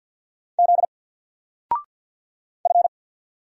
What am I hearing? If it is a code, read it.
Morse code, heard as a keyed tone, about 48 wpm, 700 Hz. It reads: Z F
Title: Single Character (Letter:Number:Punctuation:Prosign) Alt 50wpm